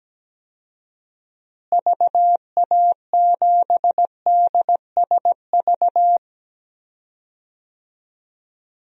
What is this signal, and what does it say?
Morse code, heard as a keyed tone, about 17 wpm, 700 Hz. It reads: VA7DSV
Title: DXCC-100 Call Signs - 6 Characters - Repeated 3x - 17wpm